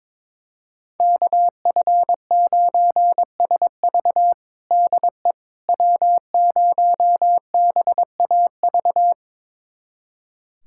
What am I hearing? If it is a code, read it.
Morse code, heard as a keyed tone, about 22 wpm, 700 Hz. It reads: KF9SV DE W0BAV
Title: USA Callsign DE Callsign - 5 Characters - 22wpm